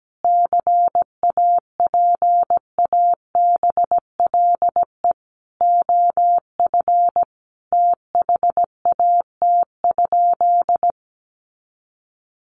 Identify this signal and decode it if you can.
Morse code, heard as a keyed tone, about 17 wpm, 700 Hz. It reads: CAPABLE OF THAT?
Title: Sentences from Top 1200 Words 17wpm